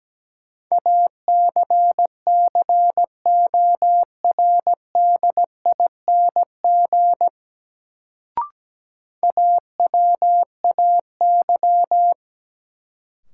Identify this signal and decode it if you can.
Morse code, heard as a keyed tone, about 17 wpm, 700 Hz. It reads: ACCORDING AWAY
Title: Sets of 2 Words - New Word Drill - 17wpm